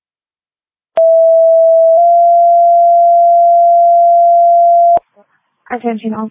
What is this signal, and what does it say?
Two-tone page, heard as a one-second tone, then a three-second tone, about 660 then 690 Hz, with speech. It is heard from 1.0 to 5.0 s.